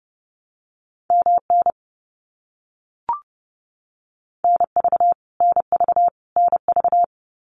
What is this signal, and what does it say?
Morse code, heard as a keyed tone, about 30 wpm, 700 Hz. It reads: MN NV NV NV
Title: US States Repeated 3x 30wpm